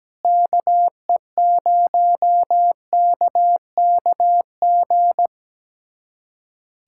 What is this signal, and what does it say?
Morse code, heard as a keyed tone, about 17 wpm, 700 Hz. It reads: KE0KKG
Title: USA Callsign DE Callsign - 6 Characters - 17wpm